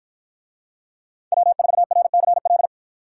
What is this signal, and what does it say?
Morse code, heard as a keyed tone, about 53 wpm, 700 Hz. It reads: W4RCL